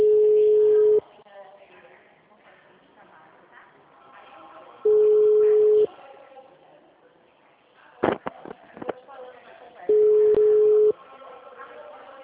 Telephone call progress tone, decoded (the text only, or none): ringback tone